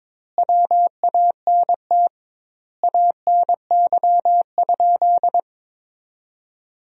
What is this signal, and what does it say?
Morse code, heard as a keyed tone, about 22 wpm, 700 Hz. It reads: WANT ANY?
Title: 2x Word-Spacing - Sentences from Top 300 Words - 22wpm